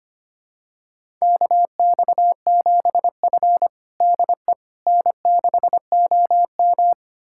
Morse code, heard as a keyed tone, about 25 wpm, 700 Hz. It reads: KX7F DE N6OM
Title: USA Callsign DE Callsign - 4 Characters - 25wpm